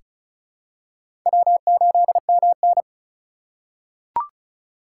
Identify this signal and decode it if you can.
Morse code, heard as a keyed tone, about 34 wpm, 700 Hz. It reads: W8MN
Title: USA Call Signs - 4 Characters - 8x Character Spacing - 35wpm